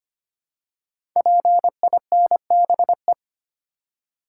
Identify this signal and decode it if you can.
Morse code, heard as a keyed tone, about 25 wpm, 700 Hz. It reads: PINBE
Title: Code Groups - 5 Letter:Numbers - 25wpm